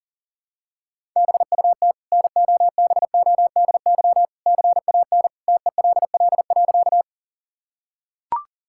Morse code, heard as a keyed tone, about 40 wpm, 700 Hz. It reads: BUT NOBODY CAN TELL.
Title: Sentences from Top 1600 Words 40wpm